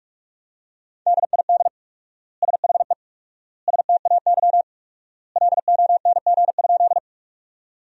Morse code, heard as a keyed tone, about 45 wpm, 700 Hz. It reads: DID SHE STAY LONG?
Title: 4x Word-Spacing - Sentences from Top 500 Words - 45wpm